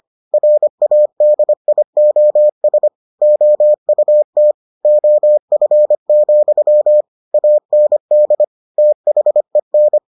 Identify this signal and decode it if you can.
Morse code, heard as a keyed tone, about 25 wpm, 600 Hz. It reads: RADIOS OUT OF, AND THEN